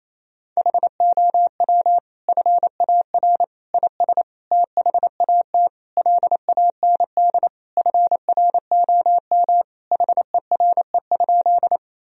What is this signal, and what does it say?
Morse code, heard as a keyed tone, about 28 wpm, 700 Hz. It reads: HOW FAR IS THAT LAND FROM HERE?
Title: Sentences from Top 1000 Words - Review Part 1 - 28wpm